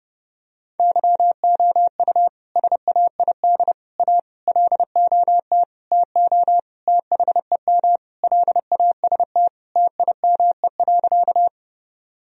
Morse code, heard as a keyed tone, about 30 wpm, 700 Hz. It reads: YOU SAID A LOT TO THEM LAST TIME.
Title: Sentences from Top 1000 Words - Review Part 3 - 30wpm